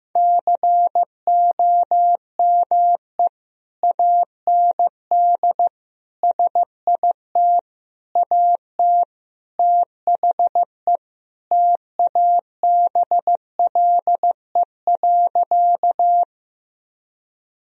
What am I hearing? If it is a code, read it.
Morse code, heard as a keyed tone, about 15 wpm, 700 Hz. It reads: COME AND SIT AT THE TABLE.
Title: Sentences from Top 1000 Words - Review Part 2 - 15wpm